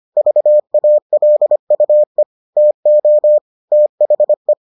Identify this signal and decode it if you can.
Morse code, heard as a keyed tone, about 25 wpm, 600 Hz. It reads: VALUE TO THE